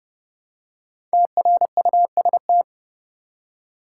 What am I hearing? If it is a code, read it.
Morse code, heard as a keyed tone, about 30 wpm, 700 Hz. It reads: TRUST
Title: Five Letter Words - 30wpm